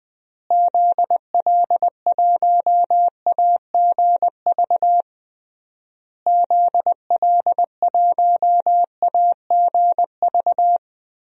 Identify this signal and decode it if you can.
Morse code, heard as a keyed tone, about 20 wpm, 700 Hz. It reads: ZL1AGV ZL1AGV
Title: DXCC-100 Call Signs - 6 Characters - Repeated 3x - 20wpm